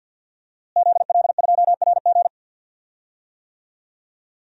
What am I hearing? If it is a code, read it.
Morse code, heard as a keyed tone, about 52 wpm, 700 Hz. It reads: ZL2RG
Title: DXCC-100 Call Signs - 5 Characters - Repeated 3x - 50wpm